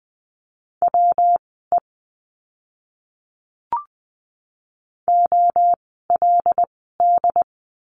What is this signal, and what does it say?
Morse code, heard as a keyed tone, about 20 wpm, spaced as for 14 wpm, 700 Hz. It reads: WE OLD
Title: 192 Lesson 31f - Letter v - Words 20wpm